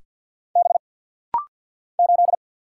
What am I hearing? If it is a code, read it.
Morse code, heard as a keyed tone, about 50 wpm, 700 Hz. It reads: B 8
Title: Single Letter-Number Alt - Rapid-Fire 50wpm